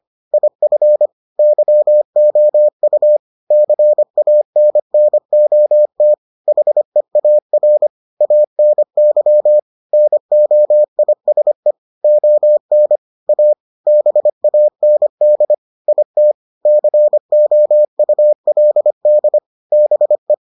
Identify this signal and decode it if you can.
Morse code, heard as a keyed tone, about 25 wpm, 600 Hz. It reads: IF YOU CANNOT HEAR ANY NOISE ON A BAND IT COULD BE